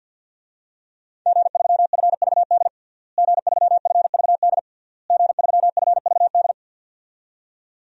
Morse code, heard as a keyed tone, about 50 wpm, 700 Hz. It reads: G3FVD G3FVD G3FVD
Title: DXCC-100 Call Signs - 5 Characters - Repeated 3x - 50wpm